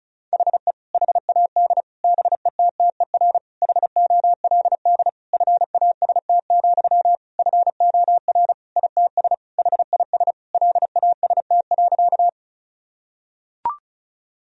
Morse code, heard as a keyed tone, about 36 wpm, 700 Hz. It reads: HE HAD BETTER HOLD FAST, FOR ITS HIS LAST.
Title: Sentences from Top 1000 Words - Review Part 2 - 35wpm